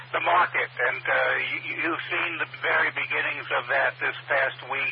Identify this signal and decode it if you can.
Emergency Alert System (SAME) tones, heard as harsh buzzing bursts, with speech. It is heard from 0.9 to 4.2 s.